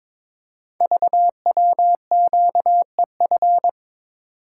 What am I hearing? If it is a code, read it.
Morse code, heard as a keyed tone, about 23 wpm, 700 Hz. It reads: VWQEF
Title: Code Groups - 5 Letters - 22wpm